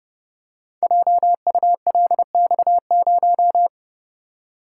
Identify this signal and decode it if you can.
Morse code, heard as a keyed tone, about 30 wpm, 700 Hz. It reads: JULX0
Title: Code Groups - 5 Letter:Numbers - 30wpm